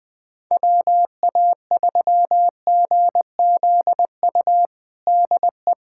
Morse code, heard as a keyed tone, about 20 wpm, 700 Hz. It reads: WA3GZU DE
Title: USA Callsign DE Callsign - 6 Characters - 20wpm